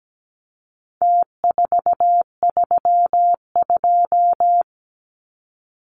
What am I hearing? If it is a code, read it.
Morse code, heard as a keyed tone, about 17 wpm, 700 Hz. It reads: T432
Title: Numbers - Contest Style - 4-Digit 17wpm